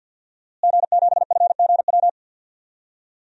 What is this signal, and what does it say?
Morse code, heard as a keyed tone, about 48 wpm, 700 Hz. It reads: G7FGW
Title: DXCC-100 Call Signs - 5 Characters - Repeated 2x - 50wpm